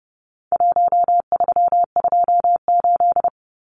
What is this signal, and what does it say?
Morse code, heard as a keyed tone, about 30 wpm, 700 Hz. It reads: 1328
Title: Numbers - Contest Style - 4-Digit 30wpm